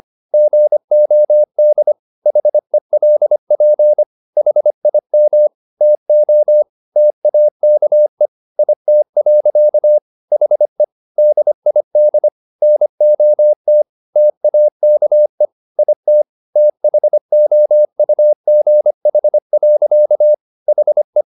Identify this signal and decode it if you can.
Morse code, heard as a keyed tone, about 25 wpm, 600 Hz. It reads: GOD HELP HIM TO TAKE IT. HE DID NOT TAKE IT THOUGH. HE
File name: the old man and the sea part 1 25wpm